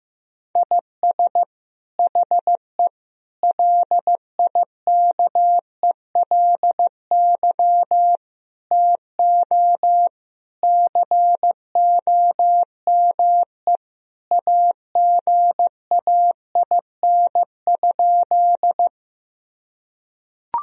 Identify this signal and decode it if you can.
Morse code, heard as a keyed tone, about 16 wpm, 700 Hz. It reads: IS HE LIKELY TO COME AGAIN?
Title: Sentences from Top 1000 Words - Review Part 4 - 15wpm